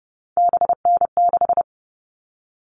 Morse code, heard as a keyed tone, about 30 wpm, 700 Hz. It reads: BN6
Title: All Cut Numbers - 3-Digit 30wpm